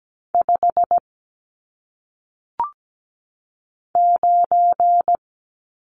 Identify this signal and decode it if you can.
Morse code, heard as a keyed tone, about 17 wpm, 700 Hz. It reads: 5 9
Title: Numbers - 1-Digit 17wpm